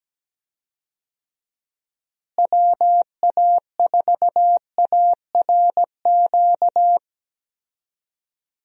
Morse code, heard as a keyed tone, about 17 wpm, 700 Hz. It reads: WA4ARQ
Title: USA Call Signs - 6 Characters - Repeated 3x - 17wpm